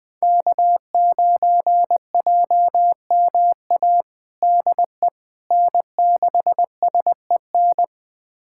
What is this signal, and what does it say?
Morse code, heard as a keyed tone, about 20 wpm, 700 Hz. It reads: K9JMA DE N6SEN
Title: USA Callsign DE Callsign - 5 Characters - 20wpm